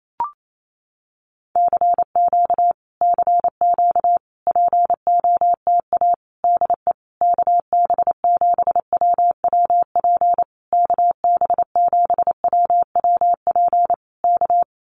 Morse code, heard as a keyed tone, about 28 wpm, 700 Hz. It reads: CQ CQ POTA DE KB7WWP KB7WWP K
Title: Common QSO Phrases Encore 28wpm